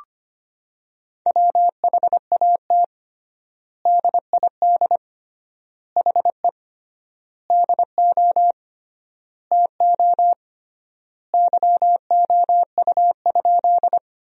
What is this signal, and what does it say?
Morse code, heard as a keyed tone, about 25 wpm, 700 Hz. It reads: WHAT DID HE DO TO YOU?
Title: 3x Word-Spacing - Sentences from Top 100 Words - 25wpm